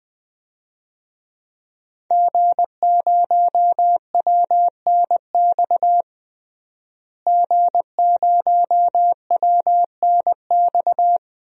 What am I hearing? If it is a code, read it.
Morse code, heard as a keyed tone, about 21 wpm, 700 Hz. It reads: G0WNX G0WNX
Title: DXCC-100 Call Signs - 5 Characters - Repeated 3x - 20wpm